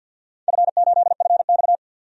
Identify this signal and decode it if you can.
Morse code, heard as a keyed tone, about 51 wpm, 700 Hz. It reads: F8FX
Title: DXCC-100 Call Signs - 4 Characters - 50wpm